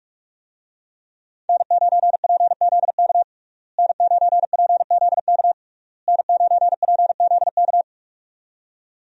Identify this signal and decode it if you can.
Morse code, heard as a keyed tone, about 46 wpm, 700 Hz. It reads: N9PZK N9PZK N9PZK
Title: USA Call Signs - 5 Characters - Repeated 3x - 45wpm